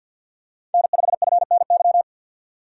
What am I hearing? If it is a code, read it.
Morse code, heard as a keyed tone, about 45 wpm, 700 Hz. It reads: N5FNY
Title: USA Call Signs - 5 Characters - 4x Character Spacing - 50wpm